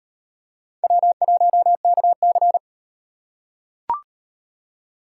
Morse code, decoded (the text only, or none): W1KC